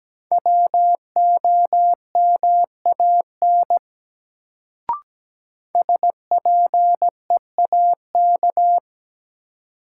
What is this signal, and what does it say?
Morse code, heard as a keyed tone, about 17 wpm, 700 Hz. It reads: WOMAN SPEAK
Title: Sentences from Top 1000 Words - Review Part 2 - New Word Drill - 17wpm